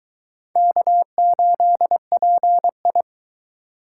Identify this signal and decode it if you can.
Morse code, heard as a keyed tone, about 22 wpm, 700 Hz. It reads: K8PI